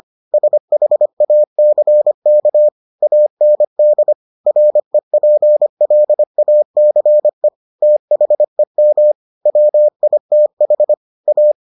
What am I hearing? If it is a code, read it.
Morse code, heard as a keyed tone, about 25 wpm, 600 Hz. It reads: SHACK AND REPLACE THEM WITH A